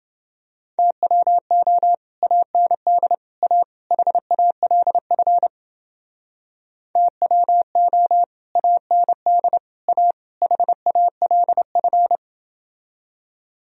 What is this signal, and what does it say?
Morse code, heard as a keyed tone, about 29 wpm, 700 Hz. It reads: TWO AND A HALF TWO AND A HALF